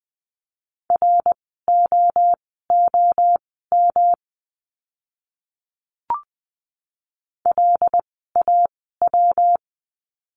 Morse code, heard as a keyed tone, about 20 wpm, spaced as for 15 wpm, 700 Hz. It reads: ROOM LAW
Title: 115 Lesson 19f - Letter w - Words 20wpm